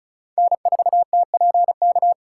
Morse code, heard as a keyed tone, about 36 wpm, 700 Hz. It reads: N4TPK
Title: USA Call Signs - 4x Character Spacing - 35wpm